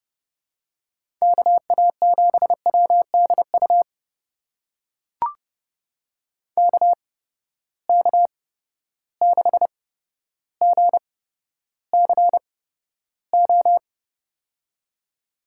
Morse code, heard as a keyed tone, about 30 wpm, spaced as for 13 wpm, 700 Hz. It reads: KA7WDU KK6GCO
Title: USA Call Signs - 6 Characters - 8x Character Spacing - 30wpm